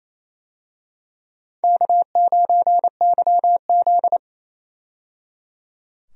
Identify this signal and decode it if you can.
Morse code, heard as a keyed tone, about 28 wpm, 700 Hz. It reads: K9YZ
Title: USA Call Signs - 4 Characters - 28wpm